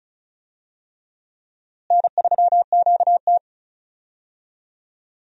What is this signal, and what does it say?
Morse code, heard as a keyed tone, about 36 wpm, 700 Hz. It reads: N3QT
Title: USA Call Signs - 4 Characters - Repeated 3x - 35wpm